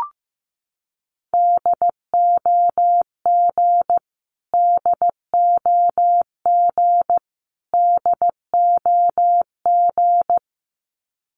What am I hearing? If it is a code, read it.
Morse code, heard as a keyed tone, about 15 wpm, 700 Hz. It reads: DOG DOG DOG
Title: Top 500 Words Repeated 15wpm